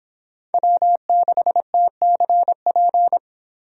Morse code, heard as a keyed tone, about 26 wpm, 700 Hz. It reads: W6TCP